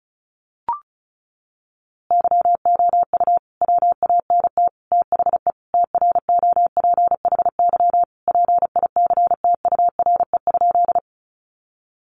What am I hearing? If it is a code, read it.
Morse code, heard as a keyed tone, about 34 wpm, 700 Hz. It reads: YOU WANT THE TROPHY PICTURE?
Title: A Sound Of Thunder 35wpm